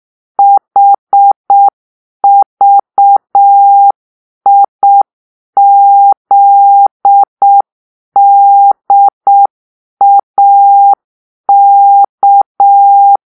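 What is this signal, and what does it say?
Morse code, heard as a keyed tone, about 6 wpm, 800 Hz. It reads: HVIZDAK
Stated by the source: Morse-code cipher made by a player team for the Kachna puzzle game.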